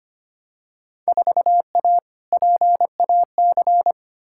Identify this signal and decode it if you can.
Morse code, heard as a keyed tone, about 25 wpm, 700 Hz. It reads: 4A PAC
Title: Field Day Exchanges Encore - 25wpm